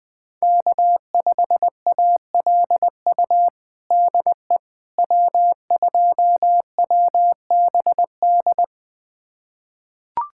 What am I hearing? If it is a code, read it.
Morse code, heard as a keyed tone, about 20 wpm, 700 Hz. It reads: K5ALU DE W2WBD
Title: USA Callsign DE Callsign - 5 Characters - 20wpm